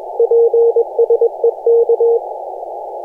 Morse code, heard as a keyed tone, about 21 wpm, 470 Hz. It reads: PSE K
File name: posts / 2016-jan-16-2256z-e51j-18-090mhz-cw-cqing